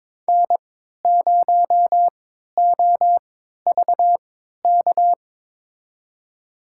Morse code, heard as a keyed tone, about 22 wpm, spaced as for 12 wpm, 700 Hz. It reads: N0OVK
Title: USA Call Signs - 3x Character Spacing - 22wpm